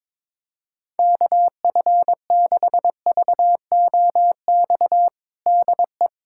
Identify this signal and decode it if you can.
Morse code, heard as a keyed tone, about 22 wpm, 700 Hz. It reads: KF6VOX DE
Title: USA Callsign DE Callsign - 6 Characters - 22wpm